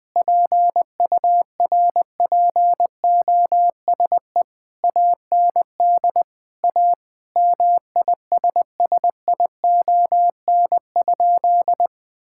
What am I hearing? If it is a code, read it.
Morse code, heard as a keyed tone, about 21 wpm, 700 Hz. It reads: PURPOSE AND A MISSION?
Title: Sentences from Top 1600 Words 20wpm